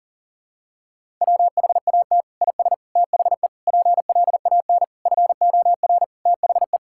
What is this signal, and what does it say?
Morse code, heard as a keyed tone, about 40 wpm, 700 Hz. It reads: WHAT IS THE PLAN FOR THE
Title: Sentences from Top 1000 Words - Review Part 5 - 40wpm